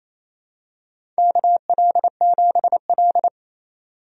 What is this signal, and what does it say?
Morse code, heard as a keyed tone, about 29 wpm, 700 Hz. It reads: KL7L
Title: DXCC-100 Call Signs - 4 Characters - 28wpm